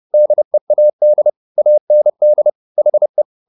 Morse code, heard as a keyed tone, about 30 wpm, 600 Hz. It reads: DEAD AND HE